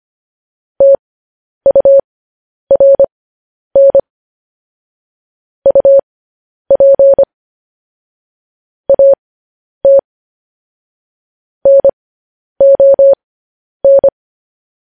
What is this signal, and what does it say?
Morse code, heard as a keyed tone, about 25 wpm, spaced as for 10 wpm, 550 Hz. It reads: TURN UP AT NON
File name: podcast / News headlines-1x-25x10